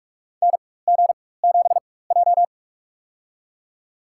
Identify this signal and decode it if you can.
Morse code, heard as a keyed tone, about 46 wpm, spaced as for 21 wpm, 700 Hz. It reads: NG7J
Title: USA Call Signs - 4 Characters - 4x Character Spacing - 45wpm